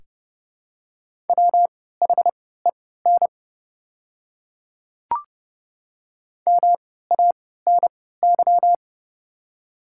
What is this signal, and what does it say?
Morse code, heard as a keyed tone, about 30 wpm, spaced as for 17 wpm, 690 Hz. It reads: WHEN MANY
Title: 3x Character Spacing - Top 300 Words - 30wpm